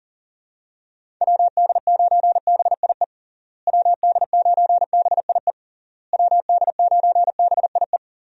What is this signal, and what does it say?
Morse code, heard as a keyed tone, about 40 wpm, 700 Hz. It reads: WD9BIE WD9BIE WD9BIE
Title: USA Call Signs - 6 Characters - Repeated 3x - 40wpm